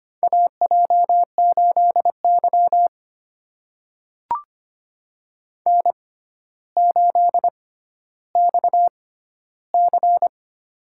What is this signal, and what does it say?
Morse code, heard as a keyed tone, about 25 wpm, spaced as for 13 wpm, 700 Hz. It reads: AJ8Y N8XC
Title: USA Call Signs - 6x Character Spacing - 25wpm